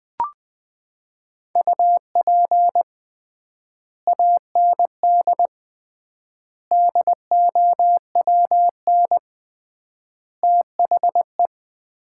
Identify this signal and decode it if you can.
Morse code, heard as a keyed tone, about 20 wpm, 700 Hz. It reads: UP AND DOWN THE
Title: 3x Word-Spacing - Sets of 4 Words - 20wpm